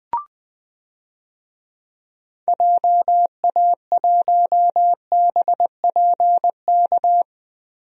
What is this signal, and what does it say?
Morse code, heard as a keyed tone, about 20 wpm, 700 Hz. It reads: JA1BPK
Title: DXCC-100 Call Signs - 6 Characters - Repeated 3x - 20wpm